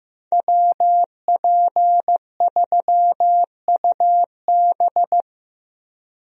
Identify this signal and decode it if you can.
Morse code, heard as a keyed tone, about 15 wpm, 700 Hz. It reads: WP3UB
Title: USA Call Signs - 5 Characters - Repeated 2x - 15wpm